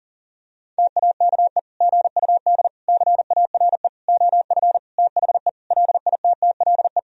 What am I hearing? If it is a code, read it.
Morse code, heard as a keyed tone, about 37 wpm, 700 Hz. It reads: TAKE GUD CARE OF THE LITTLE
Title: Sentences from Top 400 Words and Abbreviations 40wpm